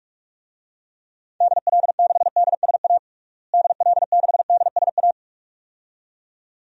Morse code, heard as a keyed tone, about 44 wpm, 700 Hz. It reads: DL6DSA DL6DSA
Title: DXCC-100 Call Signs - 6 Characters - Repeated 2x - 45wpm